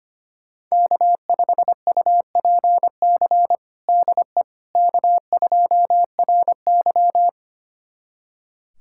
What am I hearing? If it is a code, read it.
Morse code, heard as a keyed tone, about 25 wpm, 700 Hz. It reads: K5UPC DE K2RY
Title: USA Callsign DE Callsign 25wpm